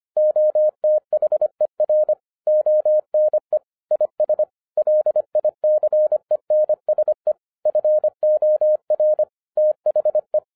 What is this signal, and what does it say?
Morse code, heard as a keyed tone, about 25 wpm, 610 Hz. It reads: OTHER ONE IS LICENSE FOR THE
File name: 2012-01-mp3 / n08-ITS MY FAULT, BUT-25wpm